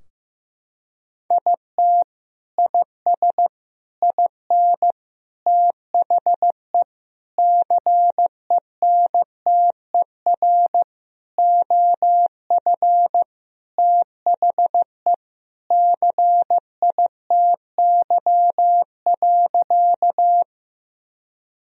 Morse code, heard as a keyed tone, about 15 wpm, 700 Hz. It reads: IT IS IN THE CENTER OF THE CITY.
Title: Sentences from Top 1000 Words - Review Part 2 - 15wpm